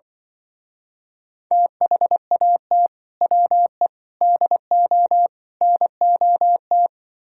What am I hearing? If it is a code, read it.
Morse code, heard as a keyed tone, about 24 wpm, 700 Hz. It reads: THAT WE DO NOT
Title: Speed-Racing - Sets of 4 Words 20wpm